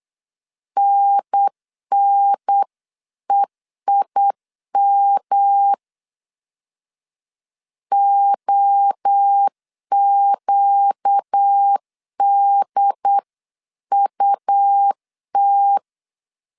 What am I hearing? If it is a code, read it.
Morse code, heard as a keyed tone, about 8 wpm, 800 Hz. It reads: NNEIM OQDUT
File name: 8wpm